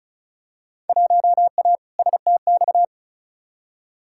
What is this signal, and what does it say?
Morse code, heard as a keyed tone, about 34 wpm, 700 Hz. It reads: 1A STX
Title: Field Day Exchanges - 35wpm